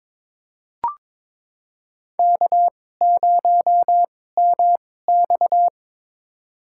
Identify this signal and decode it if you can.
Morse code, heard as a keyed tone, about 22 wpm, spaced as for 16 wpm, 700 Hz. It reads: K0MX
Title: USA Call Signs - 4 Characters - 2x Character Spacing - 22wpm